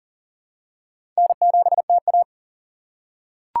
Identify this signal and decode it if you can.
Morse code, heard as a keyed tone, about 39 wpm, 690 Hz. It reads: N7TA
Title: USA Call Signs - 4 Characters - 6x Character Spacing - 40wpm